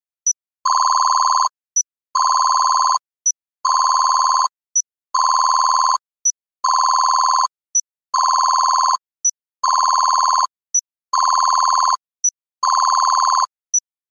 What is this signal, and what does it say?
There is no code.